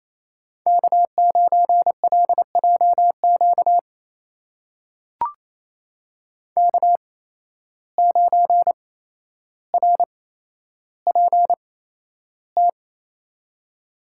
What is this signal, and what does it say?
Morse code, heard as a keyed tone, about 28 wpm, spaced as for 12 wpm, 700 Hz. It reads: K9LJQ K9RPT
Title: USA Call Signs - 8x Character Spacing - 28wpm